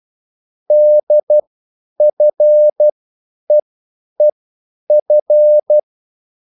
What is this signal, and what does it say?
Morse code, heard as a keyed tone, about 12 wpm, spaced as for 9 wpm, 600 Hz. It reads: DFEEF
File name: cwbook